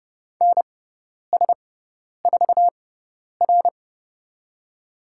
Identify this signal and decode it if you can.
Morse code, heard as a keyed tone, about 30 wpm, spaced as for 10 wpm, 700 Hz. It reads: NS4R
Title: USA Call Signs - 6x Character Spacing - 30wpm